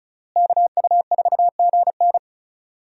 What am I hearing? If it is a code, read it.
Morse code, heard as a keyed tone, about 36 wpm, 700 Hz. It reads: KU4GN
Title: USA Call Signs - 5 Characters - Repeated 3x - 35wpm